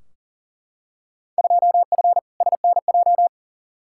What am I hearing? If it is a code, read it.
Morse code, heard as a keyed tone, about 37 wpm, 700 Hz. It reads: 2F SNJ
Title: Field Day Exchanges Encore - 40wpm